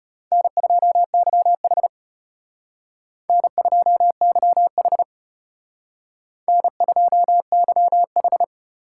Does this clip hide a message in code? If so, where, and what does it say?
Yes: Morse code from 0.3 to 8.4 s, heard as a keyed tone, about 33 wpm, 700 Hz. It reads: N2YH N2YH N2YH